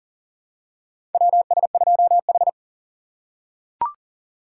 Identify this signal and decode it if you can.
Morse code, heard as a keyed tone, about 37 wpm, 700 Hz. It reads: WS2H
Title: USA Call Signs - 4 Characters - 3x Character Spacing - 40wpm